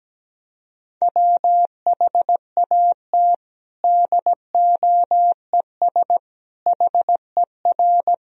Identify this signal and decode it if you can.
Morse code, heard as a keyed tone, about 17 wpm, 700 Hz. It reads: WHAT DOES HER
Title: Sentences from Top 1000 Words - Review Part 4 - 17wpm